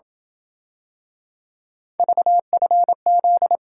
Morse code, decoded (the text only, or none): VFZ